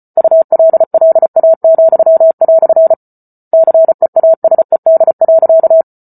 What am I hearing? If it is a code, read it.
Morse code, heard as a keyed tone, about 34 wpm, 650 Hz. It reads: ULLA," CEASED.